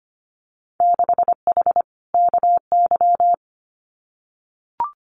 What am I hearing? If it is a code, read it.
Morse code, heard as a keyed tone, about 25 wpm, 700 Hz. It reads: 6H KY
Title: Winter Field Day Exchanges Encore 25wpm